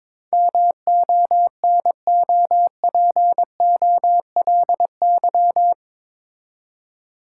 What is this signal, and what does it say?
Morse code, heard as a keyed tone, about 22 wpm, 700 Hz. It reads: MONOPOLY
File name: practice / 22wpm / Eight Letter Words - 22wpm